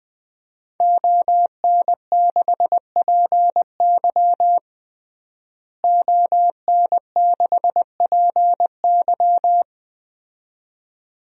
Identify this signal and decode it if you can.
Morse code, heard as a keyed tone, about 20 wpm, 700 Hz. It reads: ON6PY ON6PY
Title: DXCC-100 Call Signs - 5 Characters - Repeated 2x - 20wpm